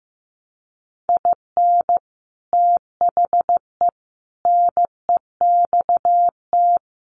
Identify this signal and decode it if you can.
Morse code, heard as a keyed tone, about 15 wpm, 700 Hz. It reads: IN THE NEXT
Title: Sets of 5 Words Encore2 15wpm